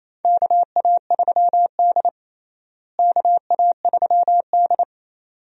Morse code, heard as a keyed tone, about 28 wpm, 700 Hz. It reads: KA3D KA3D
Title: USA Call Signs - 4 Characters - Repeated 2x - 28wpm